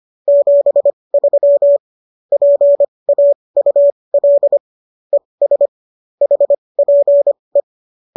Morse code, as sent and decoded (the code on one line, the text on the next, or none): --... ...-- / .--. .- ..- .-.. / . ... / .... .--. .
73 PAUL ES HPE